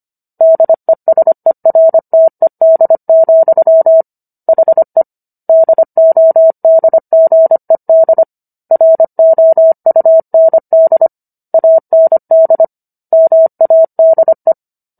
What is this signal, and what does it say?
Morse code, heard as a keyed tone, about 25 wpm, 650 Hz. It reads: DESERTED, HE DODGED ROUND AND MADE